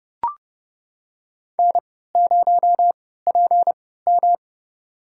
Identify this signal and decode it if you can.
Morse code, heard as a keyed tone, about 30 wpm, spaced as for 17 wpm, 700 Hz. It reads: N0PM
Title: USA Call Signs - 4 Characters - 3x Character Spacing - 30wpm